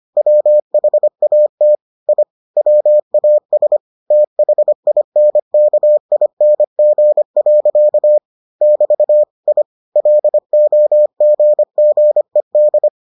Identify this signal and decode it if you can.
Morse code, heard as a keyed tone, about 25 wpm, 600 Hz. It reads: WHAT I WAS THINKING. <BT> I LOGGED